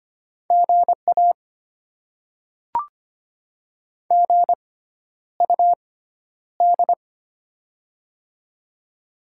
Morse code, heard as a keyed tone, about 25 wpm, spaced as for 11 wpm, 700 Hz. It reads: GA GUD
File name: Top 20 QSO Elements - 6x Character Spacing - 25wpm